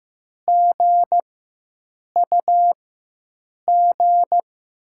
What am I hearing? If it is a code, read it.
Morse code, heard as a keyed tone, about 15 wpm, spaced as for 7 wpm, 700 Hz. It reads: GUG
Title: USA Call Signs - 6 Characters - 4x Character Spacing - 15wpm